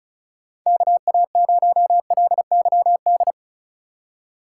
Morse code, heard as a keyed tone, about 36 wpm, 700 Hz. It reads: KA0LYD
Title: USA Call Signs - 6 Characters - 3x Character Spacing - 35wpm